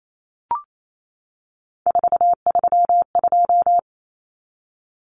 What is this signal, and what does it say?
Morse code, heard as a keyed tone, about 28 wpm, 700 Hz. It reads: 432
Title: Numbers - 3-Digit 28wpm